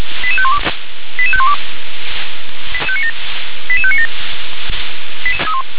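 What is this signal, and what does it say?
Five-tone selective call (ZVEI), heard as quick runs of five tones, tones of about 70 ms. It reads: 88512 88512 88597
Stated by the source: ZVEI five-tone selective call received on SDR.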